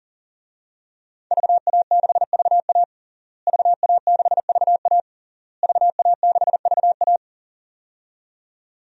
Morse code, heard as a keyed tone, about 40 wpm, 700 Hz. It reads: VA6VA VA6VA VA6VA
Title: DXCC-100 Call Signs - 5 Characters - Repeated 3x - 40wpm